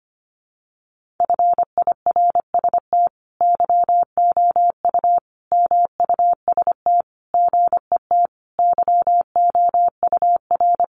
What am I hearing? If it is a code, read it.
Morse code, heard as a keyed tone, about 25 wpm, 700 Hz. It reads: FIRST YOU MUST GET YOUR
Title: Sentences from Top 1500 Words 25wpm